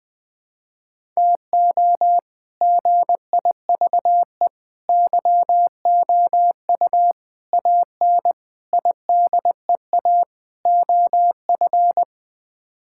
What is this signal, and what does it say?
Morse code, heard as a keyed tone, about 20 wpm, 700 Hz. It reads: TO GIVE YOU AN IDEA OF
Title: Sets of 6 Words - 20wpm